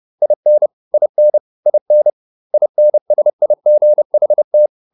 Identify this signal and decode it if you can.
Morse code, heard as a keyed tone, about 30 wpm, 600 Hz. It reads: IN IN IN INSIGHT